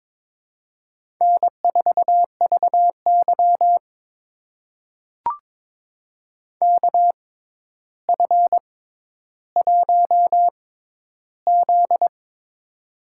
Morse code, heard as a keyed tone, about 22 wpm, spaced as for 11 wpm, 700 Hz. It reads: N4VY KF1Z
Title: USA Call Signs - 4 Characters - 6x Character Spacing - 22wpm